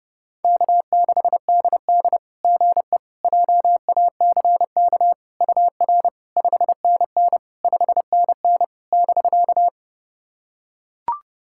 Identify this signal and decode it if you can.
Morse code, heard as a keyed tone, about 28 wpm, 660 Hz. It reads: K6DD GE JACK UR 5NN 5NN <BK>
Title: SOTA QSO Practice 30wpm